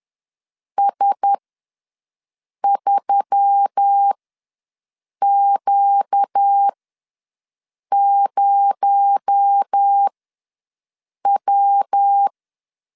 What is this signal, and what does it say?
Morse code, heard as a keyed tone, about 10 wpm, spaced as for 5 wpm, 800 Hz. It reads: S3Q0W
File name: LEZIONI / 6 PKX0 / 6 PKX0 ws 10wpm 4220